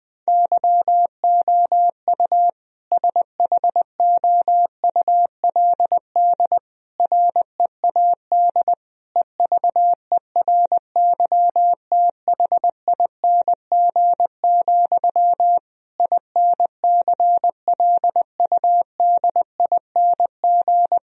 Morse code, heard as a keyed tone, about 19 wpm, 700 Hz. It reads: YOU SHOULD READ EVERYTHING, INCLUDING
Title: Sentences from Top 1000 Words - Review Part 4 - 20wpm